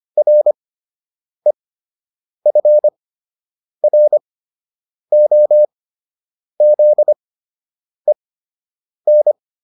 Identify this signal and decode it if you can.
Morse code, heard as a keyed tone, about 25 wpm, spaced as for 8 wpm, 600 Hz. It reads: REFROZEN